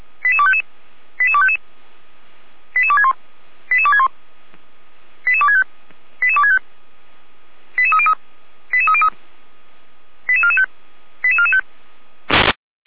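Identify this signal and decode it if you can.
Five-tone selective call (ZVEI), heard as quick runs of five tones, tones of about 70 ms. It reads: 88255 88255 88261 88261 88265 88265 88303 88303 88445 88445
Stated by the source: ZVEI five-tone selective call 88231, received on SDR.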